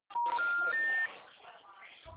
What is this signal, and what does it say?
Special information tone, with speech.